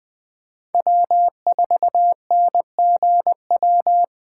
Morse code, heard as a keyed tone, about 20 wpm, 700 Hz. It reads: W4NGW
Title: USA Call Signs - 5 Characters - 4x Character Spacing - 20wpm